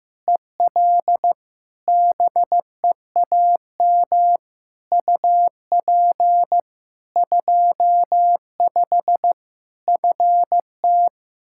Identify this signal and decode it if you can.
Morse code, heard as a keyed tone, about 15 wpm, 700 Hz. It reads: EL BEAM UP 25 FT
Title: K1USN SST QSO Practice 15wpm